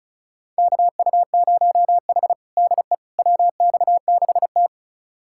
Morse code, heard as a keyed tone, about 36 wpm, 700 Hz. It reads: KU0H DE WX6T
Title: USA Callsign DE Callsign - 4 Characters - 35wpm